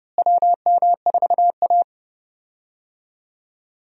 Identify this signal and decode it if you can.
Morse code, heard as a keyed tone, about 30 wpm, 700 Hz. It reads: WM4A
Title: USA Call Signs - 4 Characters - Repeated 3x - 30wpm